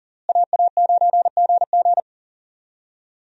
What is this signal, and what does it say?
Morse code, heard as a keyed tone, about 40 wpm, 700 Hz. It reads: AA9GG
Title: USA Call Signs - 5 Characters - Repeated 2x - 40wpm